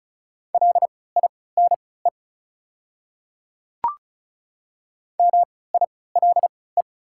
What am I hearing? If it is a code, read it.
Morse code, heard as a keyed tone, about 36 wpm, spaced as for 20 wpm, 690 Hz. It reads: LINE MILE
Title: 3x Character Spacing - Top 300 Words - 35wpm